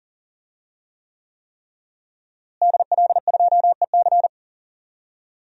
Morse code, heard as a keyed tone, about 40 wpm, 700 Hz. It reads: DL2EC
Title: DXCC-100 Call Signs - 5 Characters - Repeated 3x - 40wpm